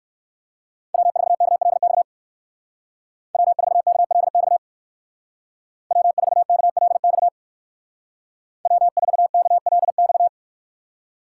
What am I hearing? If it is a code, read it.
Morse code, heard as a keyed tone, about 54 wpm, 700 Hz. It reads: W4KLX W4KLX W4KLX W4KLX
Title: Speed Racing - USA 1x3 Call Signs 45wpm